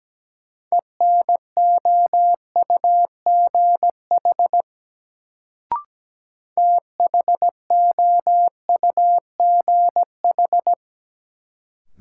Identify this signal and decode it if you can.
Morse code, heard as a keyed tone, about 17 wpm, 700 Hz. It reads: ENOUGH THOUGH
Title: Sentences from Top 400 Words - New Word Drill - 17wpm